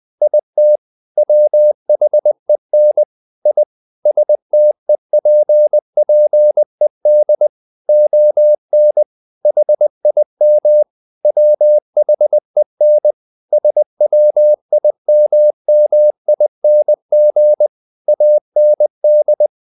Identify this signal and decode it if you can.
Morse code, heard as a keyed tone, about 20 wpm, 600 Hz. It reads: IT WHEN I STEPPED ON HIM WHEN SWIMMING AND
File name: the old man and the sea part 2 20wpm